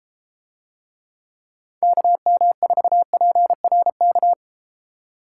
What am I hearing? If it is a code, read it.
Morse code, heard as a keyed tone, about 33 wpm, 700 Hz. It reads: KM4PRK